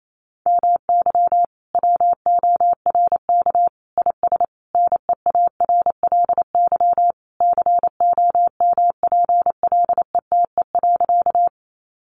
Morse code, heard as a keyed tone, about 28 wpm, 700 Hz. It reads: MY WORK IS NEARLY COMPLETE.
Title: Sentences from Top 1300 Words 28wpm